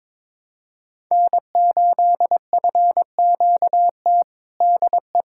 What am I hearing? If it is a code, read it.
Morse code, heard as a keyed tone, about 22 wpm, 700 Hz. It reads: N8FQT DE
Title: USA Callsign DE Callsign - 5 Characters - 22wpm